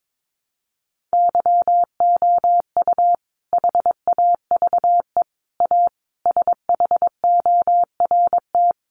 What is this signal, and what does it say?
Morse code, heard as a keyed tone, about 22 wpm, 700 Hz. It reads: YOU HAVE A SHORT
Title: Sentences from Top 1300 Words 22wpm